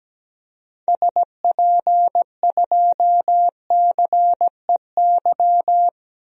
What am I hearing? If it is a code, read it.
Morse code, heard as a keyed tone, about 17 wpm, 700 Hz. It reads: SP2CEY
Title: DXCC-100 Call Signs - 6 Characters - Repeated 3x - 17wpm